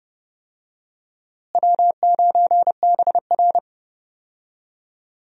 Morse code, heard as a keyed tone, about 30 wpm, 700 Hz. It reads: W9BR
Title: USA Call Signs - 4 Characters - Repeated 3x - 30wpm